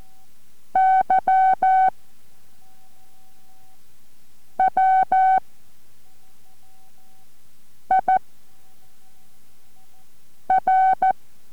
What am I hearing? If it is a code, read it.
Morse code, heard as a keyed tone, about 14 wpm, 760 Hz. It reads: Y W I R